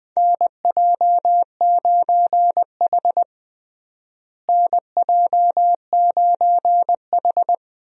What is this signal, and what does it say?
Morse code, heard as a keyed tone, about 20 wpm, 700 Hz. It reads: NJ9H NJ9H
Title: USA Call Signs - 4 Characters - Repeated 3x - 20wpm